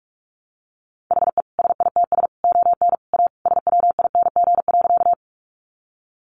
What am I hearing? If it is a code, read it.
Morse code, heard as a keyed tone, about 44 wpm, 700 Hz. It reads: HE SITS ON A SWING.
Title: Additional Easy Phrases 45wpm